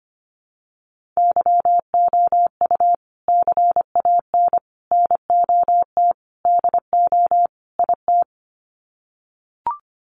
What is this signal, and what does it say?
Morse code, heard as a keyed tone, about 25 wpm, 700 Hz. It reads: YOU CAN NOT DO IT
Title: Sets of 5 Words Encore2 25wpm